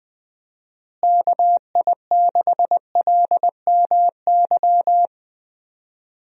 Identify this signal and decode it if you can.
Morse code, heard as a keyed tone, about 21 wpm, 700 Hz. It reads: KI6LMY
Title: USA Call Signs - 6 Characters - Repeated 3x - 20wpm